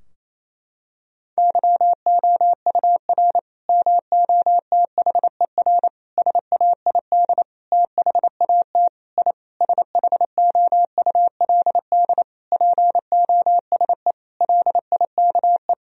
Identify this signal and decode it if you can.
Morse code, heard as a keyed tone, about 28 wpm, 700 Hz. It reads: YOUR MOTHER SAID THAT I SHOULD POSE LIKE
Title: Sentences from Top 1000 Words - Review Part 2 - 28wpm